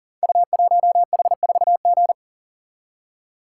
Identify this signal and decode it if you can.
Morse code, heard as a keyed tone, about 40 wpm, 690 Hz. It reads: U1H4G
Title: Code Groups - 5 Letter:Numbers - 40wpm